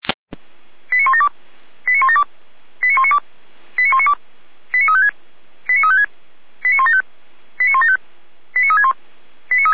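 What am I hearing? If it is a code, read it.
Five-tone selective call (ZVEI), heard as quick runs of five tones, tones of about 70 ms. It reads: 89172 89172 89192 89192 89356 89356 89175 89175 89361
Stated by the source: ZVEI five-tone selective call received by SDR, recorded on the audio server of Feuerwehr Orlamünde.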